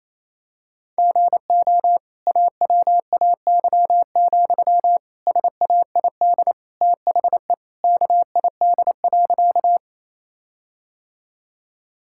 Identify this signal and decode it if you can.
Morse code, heard as a keyed tone, about 28 wpm, 700 Hz. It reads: GO AWAY, SAID THE KID.
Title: Sentences from Top 1600 Words 28wpm